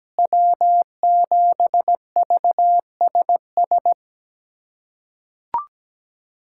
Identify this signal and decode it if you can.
Morse code, heard as a keyed tone, about 17 wpm, 700 Hz. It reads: W7VSS
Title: USA Callsign DE Callsign - 5 Characters - 17wpm